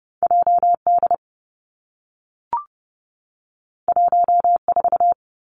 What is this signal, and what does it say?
Morse code, heard as a keyed tone, about 30 wpm, 700 Hz. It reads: J D 1 4
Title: Two Challenging Letter-Number Combinations 30wpm